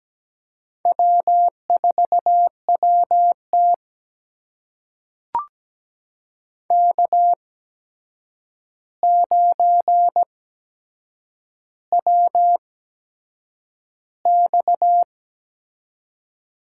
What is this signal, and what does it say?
Morse code, heard as a keyed tone, about 17 wpm, spaced as for 7 wpm, 700 Hz. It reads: W4WT K9WX
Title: USA Call Signs - 4 Characters - 8x Character Spacing - 17wpm